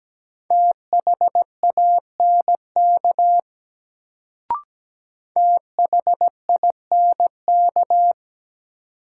Morse code, heard as a keyed tone, about 17 wpm, 700 Hz. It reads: THANK THINK
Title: Sets of 2 Words - New Word Drill - 17wpm